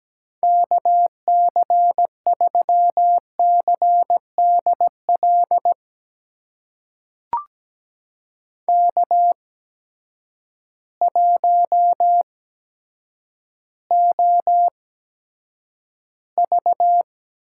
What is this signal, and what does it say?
Morse code, heard as a keyed tone, about 17 wpm, spaced as for 9 wpm, 700 Hz. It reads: KC3CDL K1OV
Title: USA Call Signs - 8x Character Spacing - 17wpm